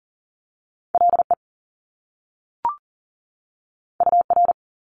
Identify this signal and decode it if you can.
Morse code, heard as a keyed tone, about 40 wpm, 700 Hz. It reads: LE UR
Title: Top Two Letter Combinations in Words 40wpm